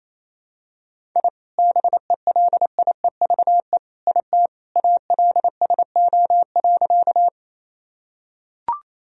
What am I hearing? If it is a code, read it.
Morse code, heard as a keyed tone, about 28 wpm, 700 Hz. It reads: I BELIEVE IT ALSO.
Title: Sentences from Top 1000 Words - Review Part 1 - 28wpm